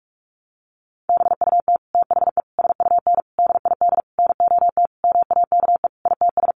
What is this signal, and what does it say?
Morse code, heard as a keyed tone, about 44 wpm, 700 Hz. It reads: BUT THE SUN DID NOT MAKE ITS
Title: Sentences from Top 1100 Words 45wpm